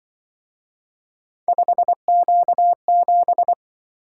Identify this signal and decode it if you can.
Morse code, heard as a keyed tone, about 24 wpm, 700 Hz. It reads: 5Q7
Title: Speed-Racing - Three Letter-Number Combinations - 20wpm